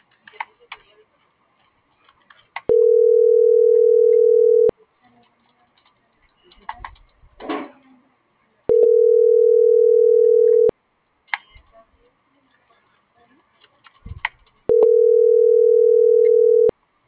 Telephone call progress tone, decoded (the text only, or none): ringback tone